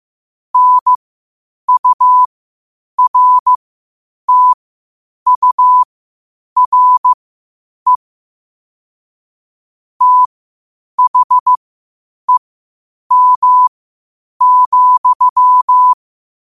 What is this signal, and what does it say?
Morse code, heard as a keyed tone, about 15 wpm, spaced as for 8 wpm, 1000 Hz. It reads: NURTURE THEM,
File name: qotdfiles / 8wpm